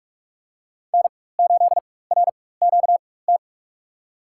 Morse code, heard as a keyed tone, about 43 wpm, spaced as for 21 wpm, 700 Hz. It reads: N8RQT
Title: USA Call Signs - 5 Characters - 4x Character Spacing - 45wpm